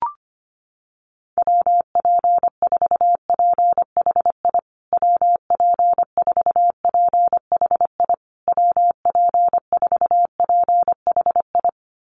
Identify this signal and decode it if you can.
Morse code, heard as a keyed tone, about 25 wpm, 700 Hz. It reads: WP4PHI WP4PHI WP4PHI
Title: USA Call Signs Repeated 3x 25wpm